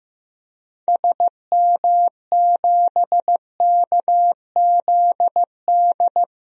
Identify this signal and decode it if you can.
Morse code, heard as a keyed tone, about 15 wpm, 700 Hz. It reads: SM7KZD
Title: DXCC-100 Call Signs - 6 Characters - 15wpm